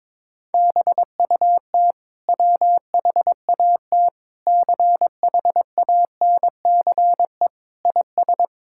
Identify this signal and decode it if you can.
Morse code, heard as a keyed tone, about 22 wpm, 700 Hz. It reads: BUT WHAT CHANCE IS
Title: Sentences from Top 1000 Words - Review Part 6 - 22wpm